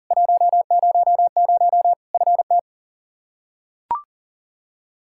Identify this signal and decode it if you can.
Morse code, heard as a keyed tone, about 40 wpm, 700 Hz. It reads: 100 FT
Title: K1USN SST QSO Practice 40wpm